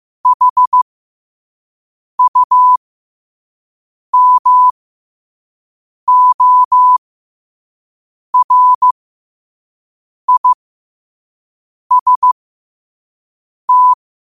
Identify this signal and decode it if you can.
Morse code, heard as a keyed tone, about 15 wpm, spaced as for 5 wpm, 1000 Hz. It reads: HUMORIST
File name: qotdfiles / 5wpm